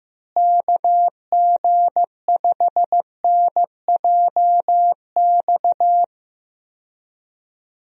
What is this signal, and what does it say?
Morse code, heard as a keyed tone, about 15 wpm, 700 Hz. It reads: KG5NJX
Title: USA Call Signs - 6 Characters - Repeated 2x - 15wpm